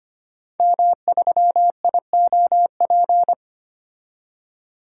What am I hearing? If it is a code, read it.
Morse code, heard as a keyed tone, about 25 wpm, 690 Hz. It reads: M3IOP
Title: DXCC-100 Call Signs - 5 Characters - 25wpm